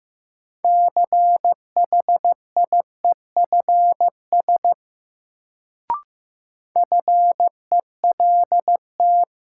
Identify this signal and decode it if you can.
Morse code, heard as a keyed tone, about 15 wpm, 700 Hz. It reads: CHIEFS FELT
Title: Sets of 5 Words Encore - New Word Drill - 15wpm